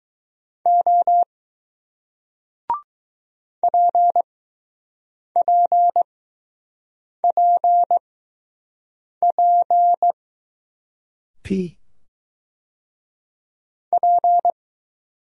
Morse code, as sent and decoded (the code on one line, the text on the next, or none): --- / .--. .--. .--. .--. / .--.
O PPPP P